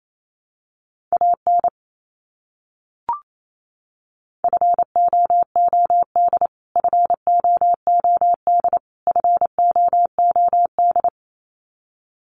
Morse code, heard as a keyed tone, about 28 wpm, 700 Hz. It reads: AN FOOD FOOD FOOD
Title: Top 300 Words Repeated 28wpm